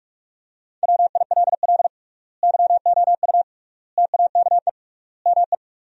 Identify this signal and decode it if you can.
Morse code, heard as a keyed tone, about 41 wpm, 700 Hz. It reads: WILL YOU TAKE ME
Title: 3x Word-Spacing - Sentences from Top 200 Words - 45wpm